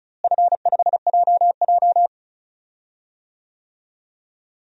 Morse code, heard as a keyed tone, about 36 wpm, 700 Hz. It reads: F5JJ
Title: DXCC-100 Call Signs - 4 Characters - Repeated 3x - 35wpm